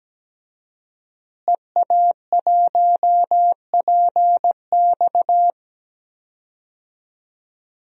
Morse code, heard as a keyed tone, about 17 wpm, 700 Hz. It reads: EA1PX
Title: DXCC-100 Call Signs - 5 Characters - Repeated 2x - 17wpm